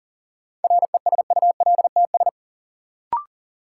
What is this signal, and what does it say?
Morse code, heard as a keyed tone, about 40 wpm, 700 Hz. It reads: RESULTS
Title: Sets of 4 Words - New Word Drill - 40wpm